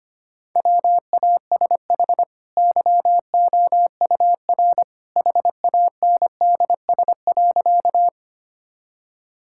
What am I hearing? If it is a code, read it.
Morse code, heard as a keyed tone, about 25 wpm, 700 Hz. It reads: WASH YOUR HANDS.
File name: Easy Sentences with Hints - 25wpm